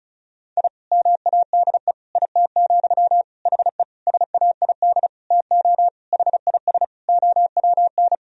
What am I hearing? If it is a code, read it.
Morse code, heard as a keyed tone, about 36 wpm, 700 Hz. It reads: I MADE IT, HE SAID TO HIS OWN
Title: Sentences from Top 1000 Words - Review Part 1 - 35wpm